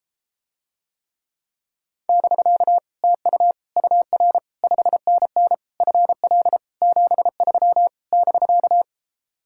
Morse code, heard as a keyed tone, about 33 wpm, 700 Hz. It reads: <BK> TU UR 5NN FL 73 <BK>